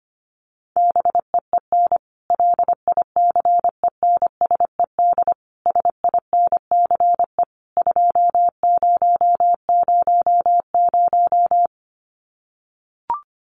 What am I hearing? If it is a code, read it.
Morse code, heard as a keyed tone, about 25 wpm, 700 Hz. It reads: BEEN LICENSED SINCE 2000
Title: Common QSO Phrases 25wpm